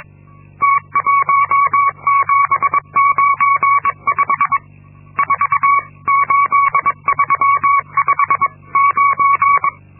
Morse code, heard as a keyed tone, about 22 wpm, 1120 Hz. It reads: T1795 48359
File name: Linksmakalnis-2